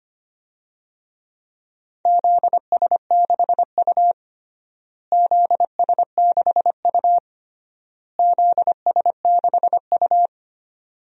Morse code, heard as a keyed tone, about 25 wpm, 700 Hz. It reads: ZS6U ZS6U ZS6U
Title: DXCC-100 Call Signs - 4 Characters - Repeated 3x - 25wpm